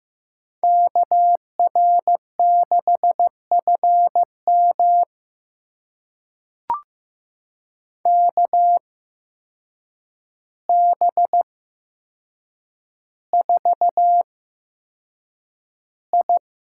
Morse code, heard as a keyed tone, about 15 wpm, spaced as for 7 wpm, 700 Hz. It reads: KR6FM KB4I
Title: USA Call Signs - 8x Character Spacing - 15wpm